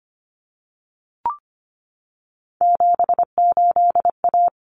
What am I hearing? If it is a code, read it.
Morse code, heard as a keyed tone, about 25 wpm, 700 Hz. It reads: 78A
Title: Common Cut Numbers - 3-Digit 25wpm